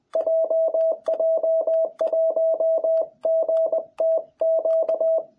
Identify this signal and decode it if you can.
Morse code, heard as a keyed tone, about 20 wpm, 660 Hz. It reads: JJ1GTQ
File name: jh8cbh / 2021cw